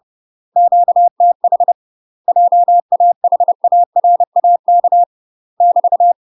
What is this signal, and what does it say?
Morse code, heard as a keyed tone, about 30 wpm, 700 Hz. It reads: QTH JAHARAK <BT>
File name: UK MASTER QSOs TRAINING / 30WPM / UK MASTER QSOs 0043